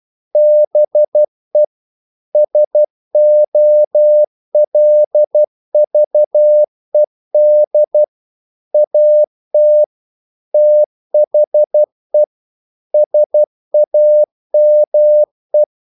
Morse code, heard as a keyed tone, about 12 wpm, 600 Hz. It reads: BE SOLVED AT THE SAME
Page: English quotations in CW (telegraphy - morse)